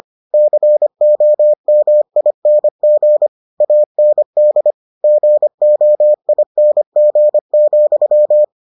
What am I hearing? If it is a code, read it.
Morse code, heard as a keyed tone, about 25 wpm, 600 Hz. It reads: COMING AND GOING,